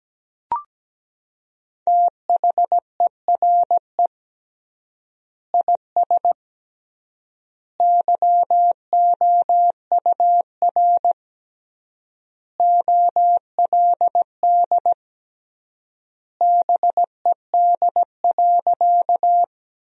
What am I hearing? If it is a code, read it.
Morse code, heard as a keyed tone, about 17 wpm, 700 Hz. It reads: THERE IS YOUR OLD BED.
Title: 3x Word-Spacing - Sentences from Top 500 Words - 17wpm